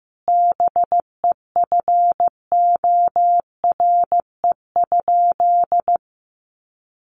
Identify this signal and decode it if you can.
Morse code, heard as a keyed tone, about 15 wpm, 700 Hz. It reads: BEFORE?
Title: Sentences from Top 1300 Words 15wpm